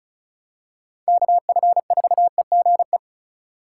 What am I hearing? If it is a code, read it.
Morse code, heard as a keyed tone, about 36 wpm, 700 Hz. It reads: KF4EGE
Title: USA Call Signs - 6 Characters - 4x Character Spacing - 35wpm